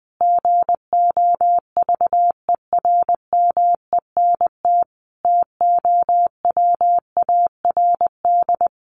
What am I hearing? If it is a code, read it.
Morse code, heard as a keyed tone, about 20 wpm, 700 Hz. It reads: GOVERMENT TOWARD
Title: Sentences from Top 1300 Words 20wpm